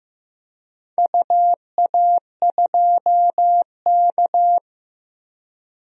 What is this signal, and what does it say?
Morse code, heard as a keyed tone, about 15 wpm, 700 Hz. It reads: UA2K
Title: DXCC-100 Call Signs - 4 Characters - 15wpm